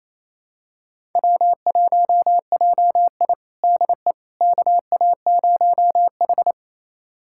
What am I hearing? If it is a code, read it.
Morse code, heard as a keyed tone, about 28 wpm, 700 Hz. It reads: W1JI DE KA0H
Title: USA Callsign DE Callsign - 4 Characters - 28wpm